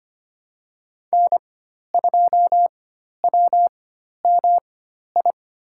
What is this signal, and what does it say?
Morse code, heard as a keyed tone, about 25 wpm, spaced as for 12 wpm, 700 Hz. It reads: N2WMI
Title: USA Call Signs - 5 Characters - 4x Character Spacing - 25wpm